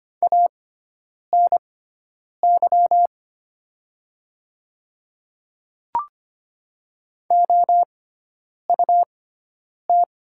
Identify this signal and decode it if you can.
Morse code, heard as a keyed tone, about 25 wpm, spaced as for 9 wpm, 700 Hz. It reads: ANY OUT
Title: 6x Character Spacing - Top 200 Words - 25wpm